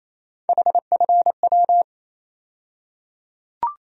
Morse code, heard as a keyed tone, about 28 wpm, 700 Hz. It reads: HFW
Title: USA Call Sign Postfixes - 3 Letters 28wpm